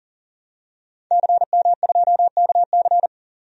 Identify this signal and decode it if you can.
Morse code, heard as a keyed tone, about 40 wpm, 700 Hz. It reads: CM2KC
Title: DXCC-100 Call Signs - 5 Characters - 40wpm